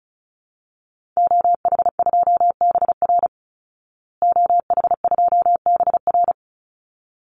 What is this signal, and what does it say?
Morse code, heard as a keyed tone, about 34 wpm, 700 Hz. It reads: OH2BR OH2BR
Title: Call Signs Repeated 3x - 35wpm